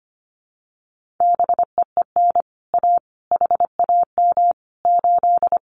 Morse code, heard as a keyed tone, about 25 wpm, 700 Hz. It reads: BEEN A HAM 8
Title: Common QSO Phrases Encore 25wpm